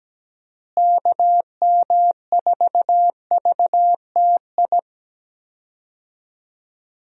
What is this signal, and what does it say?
Morse code, heard as a keyed tone, about 17 wpm, 700 Hz. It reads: KM4VTI
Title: USA Call Signs - 6 Characters - Repeated 2x - 17wpm